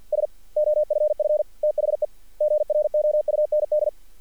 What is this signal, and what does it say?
Morse code, heard as a keyed tone, about 42 wpm, 590 Hz. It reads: S OFF THE GROUND
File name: archived / CRSnet